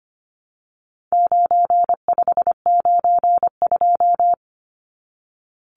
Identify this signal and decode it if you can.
Morse code, heard as a keyed tone, about 25 wpm, 700 Hz. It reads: 9592
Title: Numbers - 4-Digit 25wpm